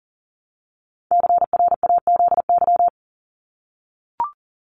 Morse code, heard as a keyed tone, about 40 wpm, 700 Hz. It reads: CRAZY
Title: Top 1501-1750 Words 40wpm